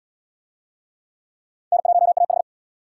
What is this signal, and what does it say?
Morse code, heard as a keyed tone, about 70 wpm, 700 Hz. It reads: N1AH